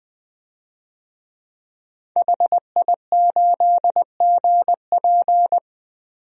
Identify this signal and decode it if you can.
Morse code, heard as a keyed tone, about 20 wpm, 700 Hz. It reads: HI8GP
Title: DXCC-100 Call Signs - 5 Characters - Repeated 2x - 20wpm